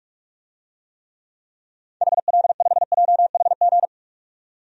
Morse code, heard as a keyed tone, about 42 wpm, 700 Hz. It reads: HL5JHG
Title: DXCC-100 Call Signs - 6 Characters - 45wpm